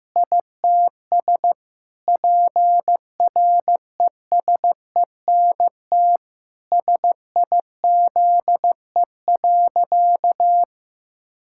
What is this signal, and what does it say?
Morse code, heard as a keyed tone, about 16 wpm, 700 Hz. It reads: ITS PRESENT SIZE.
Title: Sentences from Top 1000 Words - Review Part 6 - 15wpm